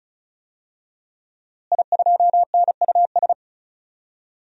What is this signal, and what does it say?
Morse code, heard as a keyed tone, about 34 wpm, 700 Hz. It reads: I2NUS
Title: DXCC-100 Call Signs - 5 Characters - Repeated 2x - 35wpm